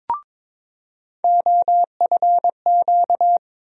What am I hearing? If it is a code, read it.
Morse code, heard as a keyed tone, about 22 wpm, 700 Hz. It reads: OFQ
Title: USA Call Sign Postfixes - 3 Letters 22wpm